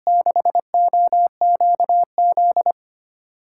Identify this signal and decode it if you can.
Morse code, heard as a keyed tone, about 25 wpm, 700 Hz. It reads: 6OQZ
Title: USA Call Signs - 6x Character Spacing - 25wpm